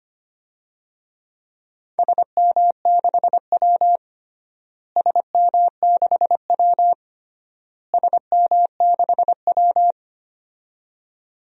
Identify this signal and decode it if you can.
Morse code, heard as a keyed tone, about 25 wpm, 700 Hz. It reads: SM6W SM6W SM6W
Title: DXCC-100 Call Signs - 4 Characters - Repeated 3x - 25wpm